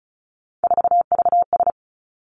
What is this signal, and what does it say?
Morse code, heard as a keyed tone, about 36 wpm, 700 Hz. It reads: 4VS
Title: Three Challenging Letter-Number Combinations 35wpm